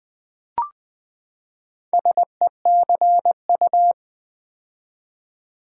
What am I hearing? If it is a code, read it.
Morse code, heard as a keyed tone, about 20 wpm, 700 Hz. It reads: SECU
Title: Build A Word - 4 Letters At A Time - 8-Letter Words - 20wpm